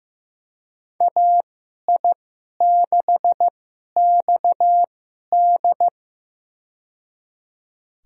Morse code, heard as a keyed tone, about 15 wpm, spaced as for 11 wpm, 700 Hz. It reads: AI6XD
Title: USA Call Signs - 5 Characters - 2x Character Spacing - 15wpm